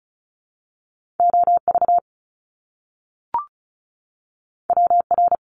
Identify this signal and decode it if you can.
Morse code, heard as a keyed tone, about 36 wpm, 700 Hz. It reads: OV WR
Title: Top Two Letter Combinations in Words 35wpm